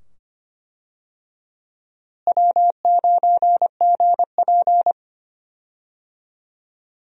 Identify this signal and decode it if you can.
Morse code, heard as a keyed tone, about 25 wpm, 700 Hz. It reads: W9GP
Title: USA Call Signs - 4 Characters - Repeated 2x - 25wpm